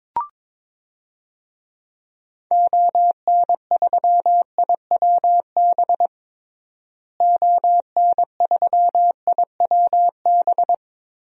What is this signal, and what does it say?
Morse code, heard as a keyed tone, about 22 wpm, 700 Hz. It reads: ON3IWB ON3IWB
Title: DXCC-100 Call Signs - 6 Characters - Repeated 3x - 22wpm